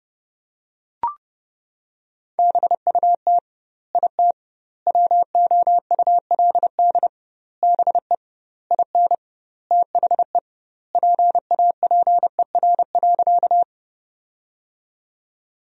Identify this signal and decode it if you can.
Morse code, heard as a keyed tone, about 31 wpm, 700 Hz. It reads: BUT IT WOULD BE IN THE PAPER.
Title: 2x Word-Spacing - Sentences from Top 300 Words - 30wpm